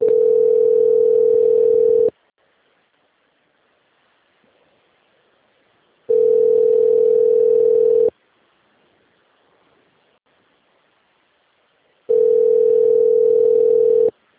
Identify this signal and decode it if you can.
Ringback tone (North America), 440 + 480 Hz.